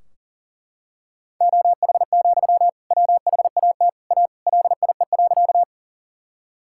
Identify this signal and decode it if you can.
Morse code, heard as a keyed tone, about 40 wpm, 700 Hz. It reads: OH, WHAT A LIE.
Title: Sentences from Top 1600 Words 40wpm